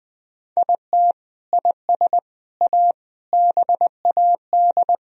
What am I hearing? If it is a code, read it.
Morse code, heard as a keyed tone, about 20 wpm, 700 Hz. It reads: IT IS A BAD
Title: Sentences from Top 1000 Words - Review Part 2 - 20wpm